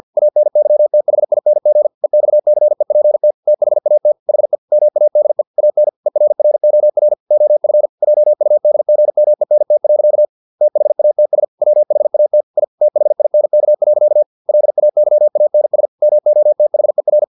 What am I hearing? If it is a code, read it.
Morse code, heard as a keyed tone, about 50 wpm, 600 Hz. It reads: ANYTHING EXCEPT THAT HE MADE AN ERROR OF JUDGMENT. THATS WHAT I THINK. LAYANS MOTHER
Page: Never 49wpm